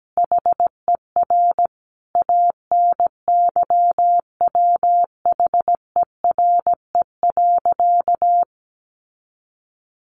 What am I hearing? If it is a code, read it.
Morse code, heard as a keyed tone, about 17 wpm, 700 Hz. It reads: HER ANYWHERE.
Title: Sentences from Top 1400 Words 17wpm